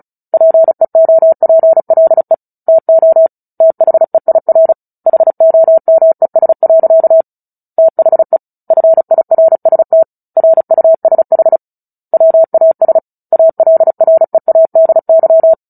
Morse code, heard as a keyed tone, about 34 wpm, 650 Hz. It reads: PEOPLE TO THEIR HOMES. THE FIRST RUSH WAS ALREADY
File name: Book Two / 09-Chapter 9 - 35 WPM